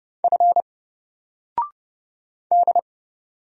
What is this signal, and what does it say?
Morse code, heard as a keyed tone, about 30 wpm, 700 Hz. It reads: F D
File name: Single-Letter-Number-30wpm